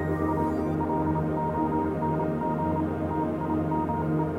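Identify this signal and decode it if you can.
Morse code, heard as a keyed tone, about 27 wpm, 940 Hz, with noise. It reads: A LONG TIME
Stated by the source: Morse code version of a music track.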